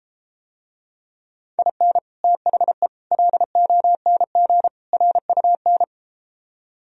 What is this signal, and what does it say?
Morse code, heard as a keyed tone, about 33 wpm, 700 Hz. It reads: IN THE LONG RUN